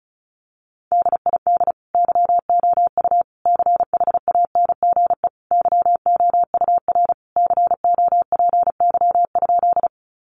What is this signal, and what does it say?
Morse code, heard as a keyed tone, about 34 wpm, 700 Hz. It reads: DID YOU CHANGE YOUR COPY?
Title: Sentences from Top 1400 Words 35wpm